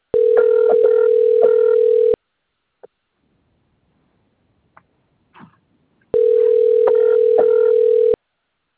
Ringback tone (North America), 440 + 480 Hz.